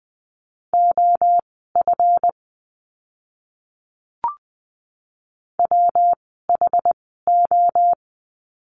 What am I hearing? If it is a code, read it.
Morse code, heard as a keyed tone, about 20 wpm, spaced as for 15 wpm, 700 Hz. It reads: OF WHO
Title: 160 Lesson 26f - Letter g - Words 20wpm